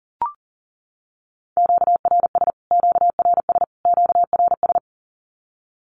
Morse code, heard as a keyed tone, about 40 wpm, 700 Hz. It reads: QRS QRS QRS
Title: Q-Codes Repeated 3x 40wpm